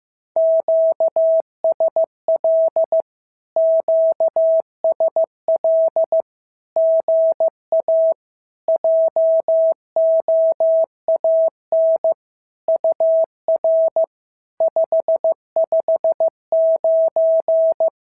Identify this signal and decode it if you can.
Morse code, heard as a keyed tone, about 15 wpm, 650 Hz. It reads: QSL QSL GA JOAN UR 559
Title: SOTA QSO Practice 15wpm